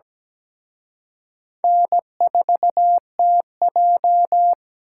Morse code, heard as a keyed tone, about 20 wpm, 700 Hz. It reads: N4TJ
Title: Speed Racing - USA Call Signs 17wpm